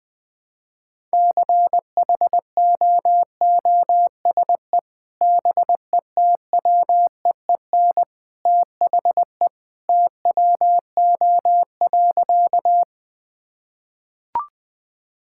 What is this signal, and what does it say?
Morse code, heard as a keyed tone, about 20 wpm, 700 Hz. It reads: CHOOSE BETWEEN THE TWO.
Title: Sentences with 2-Word Phrases - Spoken Beginning - 20wpm